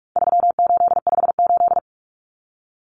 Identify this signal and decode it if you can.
Morse code, heard as a keyed tone, about 44 wpm, 700 Hz. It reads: 3858
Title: Common Cut Numbers - 4-Digit 45wpm